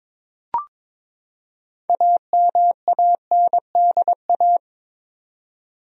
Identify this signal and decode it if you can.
Morse code, heard as a keyed tone, about 23 wpm, 700 Hz. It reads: AMANDA
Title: USA Common Names - 22wpm